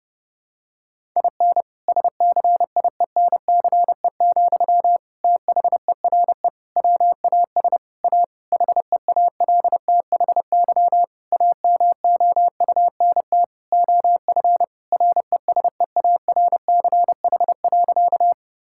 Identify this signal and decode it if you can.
Morse code, heard as a keyed tone, about 30 wpm, 700 Hz. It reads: IN SCIENCE, THERE WAS A HEALTHY AMOUNT OF RESEARCH.
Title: Sentences from Top 1000 Words - Review Part 5 - 30wpm